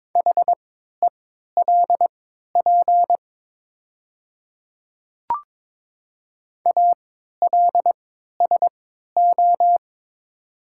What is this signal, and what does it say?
Morse code, heard as a keyed tone, about 22 wpm, spaced as for 12 wpm, 700 Hz. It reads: HELP ALSO
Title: 3x Character Spacing - Top 200 Words - 22wpm